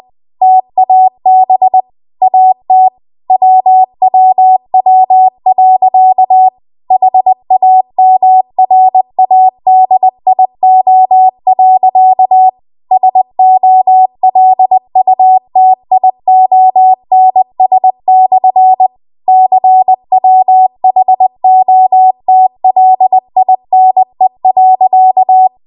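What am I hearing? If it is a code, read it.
Morse code, heard as a keyed tone, about 20 wpm, 750 Hz. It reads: TAB AT WWW.HAMRADIO.SOLUTIONS/CWHOTLINE.
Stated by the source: Morse code practice at 20 WPM.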